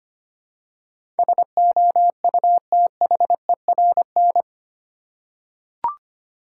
Morse code, heard as a keyed tone, about 25 wpm, 700 Hz. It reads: SOUTHERN
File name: 8 Letter Words - First 3 Letters Given - 25wpm